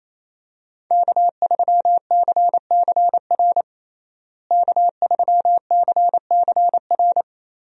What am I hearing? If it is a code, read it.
Morse code, heard as a keyed tone, about 28 wpm, 700 Hz. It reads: K3CCR K3CCR
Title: USA Call Signs - 5 Characters - Repeated 3x - 28wpm